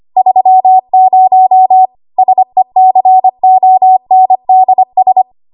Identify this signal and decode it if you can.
Morse code, heard as a keyed tone, about 25 wpm, 750 Hz. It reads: 30 SECONDS
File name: Archive / 25 WPM / 25WPM